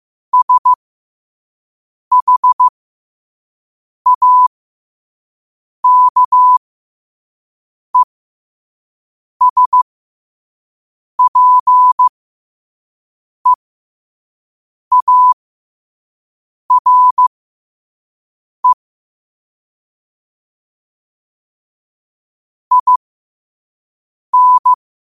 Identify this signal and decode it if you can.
Morse code, heard as a keyed tone, about 15 wpm, spaced as for 5 wpm, 1000 Hz. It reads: SHAKESPEARE IN
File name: 5wpm